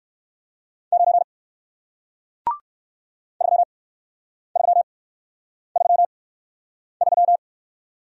Morse code, heard as a keyed tone, about 59 wpm, spaced as for 9 wpm, 700 Hz. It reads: 9 3333